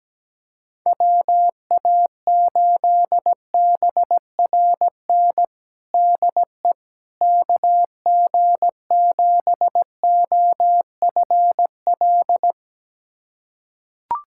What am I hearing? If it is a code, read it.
Morse code, heard as a keyed tone, about 17 wpm, 700 Hz. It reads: WA8BRN DE KG7OFL
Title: USA Callsign DE Callsign - 6 Characters - 17wpm